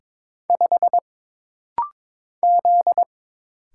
Morse code, heard as a keyed tone, about 22 wpm, 700 Hz. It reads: 5 Z
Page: This rapid-fire timing is geared to push your ICR (Instant Character Recognition) skills.